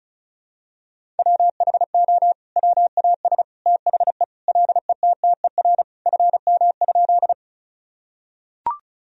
Morse code, heard as a keyed tone, about 36 wpm, 700 Hz. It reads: WHO WAS THE LETTER FM?
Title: Sentences from Top 300 Words and Abbreviations 35wpm